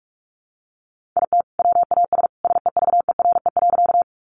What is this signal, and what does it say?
Morse code, heard as a keyed tone, about 44 wpm, 700 Hz. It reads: IT WAS SEVERE.
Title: Sentences from Top 1400 Words 45wpm